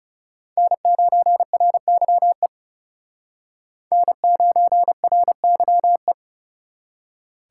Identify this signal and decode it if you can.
Morse code, heard as a keyed tone, about 36 wpm, 700 Hz. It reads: N9RYE N9RYE
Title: Speed Racing - USA 1x3 Call Signs 30wpm